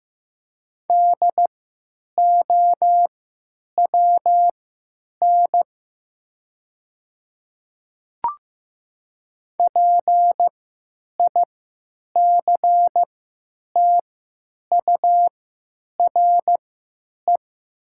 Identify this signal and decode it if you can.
Morse code, heard as a keyed tone, about 15 wpm, spaced as for 9 wpm, 700 Hz. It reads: DOWN PICTURE
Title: 3x Character Spacing - Top 200 Words - 15wpm